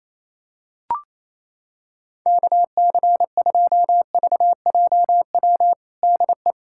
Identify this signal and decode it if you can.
Morse code, heard as a keyed tone, about 28 wpm, 700 Hz. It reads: KC2VJW DE
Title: USA Callsign DE Callsign 28wpm